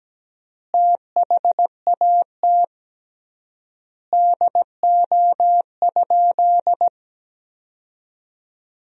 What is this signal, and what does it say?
Morse code, heard as a keyed tone, about 17 wpm, 700 Hz. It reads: THAT DO?
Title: 3x Word-Spacing - Sentences from Top 100 Words - 17wpm